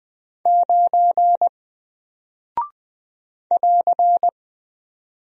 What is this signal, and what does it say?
Morse code, heard as a keyed tone, about 20 wpm, 700 Hz. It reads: 9 <AR>
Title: Single Character (Letter:Number:Punctuation:Prosign) Alt 20wpm